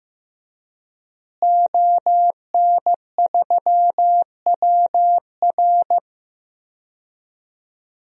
Morse code, heard as a keyed tone, about 15 wpm, 700 Hz. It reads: ON3WR
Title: DXCC-100 Call Signs - 5 Characters - Repeated 3x - 15wpm